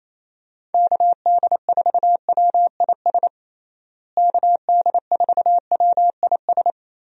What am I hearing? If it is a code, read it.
Morse code, heard as a keyed tone, about 28 wpm, 700 Hz. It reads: KD4WIS KD4WIS
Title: USA Call Signs - 6 Characters - Repeated 2x - 28wpm